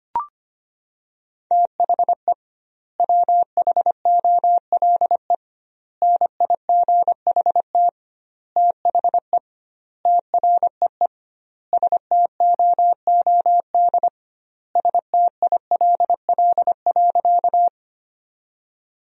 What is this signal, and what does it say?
Morse code, heard as a keyed tone, about 25 wpm, 700 Hz. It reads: THE WHOLE NIGHT THE TREE STOOD STILL.
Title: 2x Word-Spacing - Sentences from Top 500 Words - 25wpm